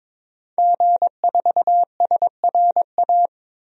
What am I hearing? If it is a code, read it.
Morse code, heard as a keyed tone, about 21 wpm, 700 Hz. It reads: G4SRA
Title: DXCC-100 Call Signs - 5 Characters - 22wpm